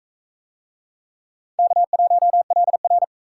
Morse code, heard as a keyed tone, about 42 wpm, 700 Hz. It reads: K1LR